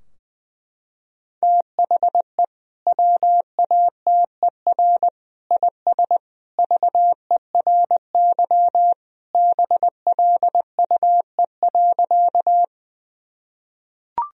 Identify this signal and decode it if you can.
Morse code, heard as a keyed tone, about 20 wpm, 700 Hz. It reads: THE WATER IS VERY BLUE.
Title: Sentences from Top 1000 Words - Review Part 3 - 20wpm